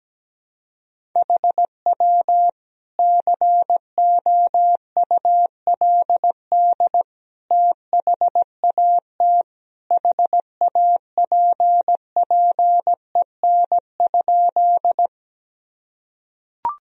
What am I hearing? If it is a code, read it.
Morse code, heard as a keyed tone, about 17 wpm, 700 Hz. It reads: HW COULD THAT HAPPEN?
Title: Sentences from Top 400 Words and Abbreviations 17wpm